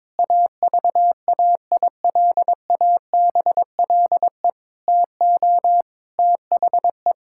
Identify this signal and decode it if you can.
Morse code, heard as a keyed tone, about 22 wpm, 700 Hz. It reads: AVAILABLE TO THE
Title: Sentences from Top 1000 Words - Review Part 6 - 22wpm